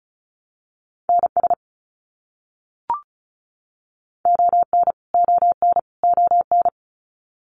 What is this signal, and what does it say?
Morse code, heard as a keyed tone, about 34 wpm, 700 Hz. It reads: NS ON ON ON
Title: Canadian Provinces and Territories - Repeated 3x - 35wpm